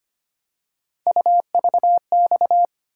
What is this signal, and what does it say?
Morse code, heard as a keyed tone, about 25 wpm, 700 Hz. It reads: UVX
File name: Three Letter-Number Combinations - 25wpm